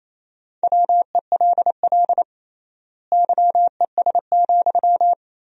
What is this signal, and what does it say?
Morse code, heard as a keyed tone, about 28 wpm, 700 Hz. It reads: WELL YES,
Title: 3x Word-Spacing - Sentences from Top 500 Words - 28wpm